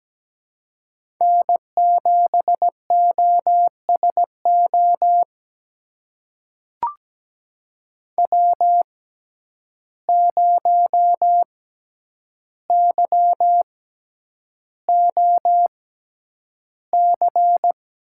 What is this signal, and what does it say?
Morse code, heard as a keyed tone, about 17 wpm, spaced as for 9 wpm, 700 Hz. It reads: N7OSO W0YOC
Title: USA Call Signs - 5 Characters - 6x Character Spacing - 17wpm